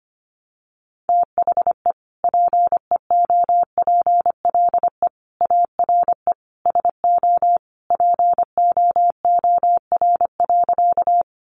Morse code, heard as a keyed tone, about 25 wpm, 700 Hz. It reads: THE PEOPLE ARE SO POOR.
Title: Sentences from Top 1000 Words 25wpm